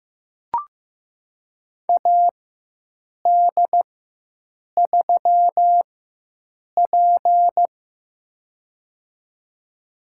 Morse code, heard as a keyed tone, about 15 wpm, spaced as for 7 wpm, 700 Hz. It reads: AD3P
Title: USA Call Signs - 4 Characters - 4x Character Spacing - 15wpm